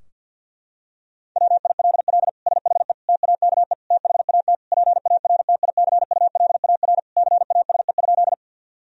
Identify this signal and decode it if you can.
Morse code, heard as a keyed tone, about 50 wpm, 700 Hz. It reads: WILL SHE TAKE THAT PARTICULAR CASE?
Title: Sentences from Top 1000 Words - Review Part 4 - 50wpm